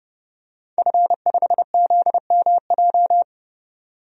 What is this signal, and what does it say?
Morse code, heard as a keyed tone, about 32 wpm, 700 Hz. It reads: F5ZMJ
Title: DXCC-100 Call Signs - 5 Characters - 30wpm